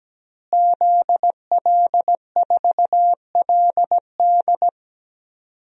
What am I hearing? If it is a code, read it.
Morse code, heard as a keyed tone, about 17 wpm, 700 Hz. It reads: ZL4LD
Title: DXCC-100 Call Signs - 5 Characters - Repeated 3x - 17wpm